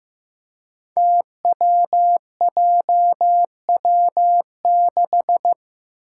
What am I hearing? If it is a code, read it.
Morse code, heard as a keyed tone, about 16 wpm, 700 Hz. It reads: TWJW6
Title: Five Challenging Letter-Number Combinations 15wpm